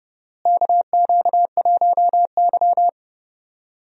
Morse code, heard as a keyed tone, about 30 wpm, 700 Hz. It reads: KQ1Y
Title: USA Call Signs - 4 Characters - Repeated 2x - 30wpm